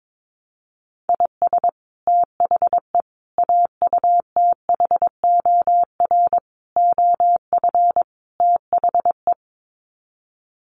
Morse code, heard as a keyed tone, about 22 wpm, 700 Hz. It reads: IS THE AUTHOR OF THE
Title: Sets of 5 Words Encore 22wpm